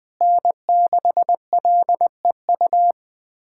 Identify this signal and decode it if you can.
Morse code, heard as a keyed tone, about 19 wpm, 700 Hz. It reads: N6LEU
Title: USA Callsign DE Callsign - 5 Characters - 20wpm